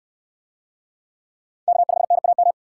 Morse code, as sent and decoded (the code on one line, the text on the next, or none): -.- ..... -. .- --
K5NAM